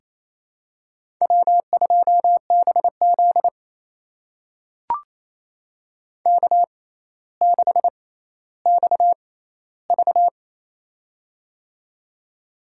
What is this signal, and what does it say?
Morse code, heard as a keyed tone, about 28 wpm, spaced as for 14 wpm, 700 Hz. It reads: W2BZ K6XV
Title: USA Call Signs - 4 Characters - 6x Character Spacing - 28wpm